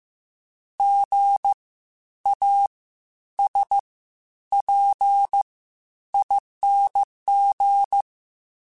Morse code, heard as a keyed tone, about 15 wpm, spaced as for 10 wpm, 800 Hz. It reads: GASPING
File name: Spaced Alphas / Spaced Alphas mp3s / Gs sp @13